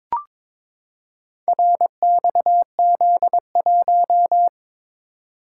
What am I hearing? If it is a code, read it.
Morse code, heard as a keyed tone, about 22 wpm, 700 Hz. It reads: RXZ1
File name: practice / 22wpm / Four Letter-Number Combinations - 22wpm